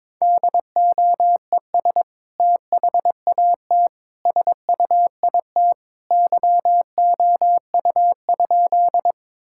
Morse code, heard as a keyed tone, about 22 wpm, 700 Hz. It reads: DOES THAT SUIT YOU?
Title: Sentences from Top 1600 Words 22wpm